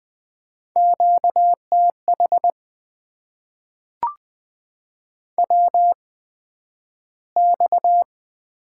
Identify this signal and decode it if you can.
Morse code, heard as a keyed tone, about 20 wpm, spaced as for 11 wpm, 700 Hz. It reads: QTH WX
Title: Top 90 QSO Elements - 8x Character Spacing - 20wpm